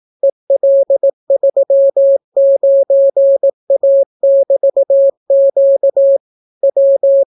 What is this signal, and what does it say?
Morse code, heard as a keyed tone, about 18 wpm, 550 Hz. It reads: EL39A<BT>Q W